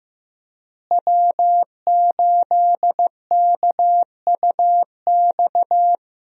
Morse code, heard as a keyed tone, about 15 wpm, 700 Hz. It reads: W8KUX
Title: USA Call Signs - 5 Characters - Repeated 2x - 15wpm